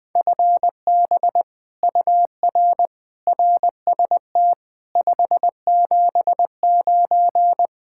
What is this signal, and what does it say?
Morse code, heard as a keyed tone, about 20 wpm, 700 Hz. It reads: FB UR RST 579
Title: Speed-Racing Common QSO Phrases - 17wpm